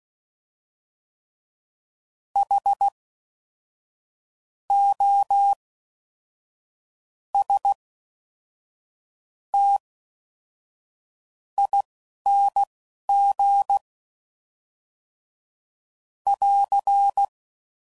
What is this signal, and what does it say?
Morse code, heard as a keyed tone, about 16 wpm, spaced as for 11 wpm, 800 Hz. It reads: H O S T ING <AR>
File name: Suffixes / Special INGs / Fast INGs